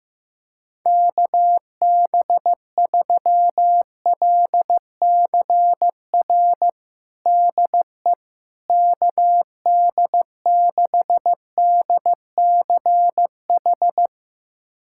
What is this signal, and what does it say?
Morse code, heard as a keyed tone, about 15 wpm, 700 Hz. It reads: KB3LCR DE KD6DCH
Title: USA Callsign DE Callsign - 6 Characters - 15wpm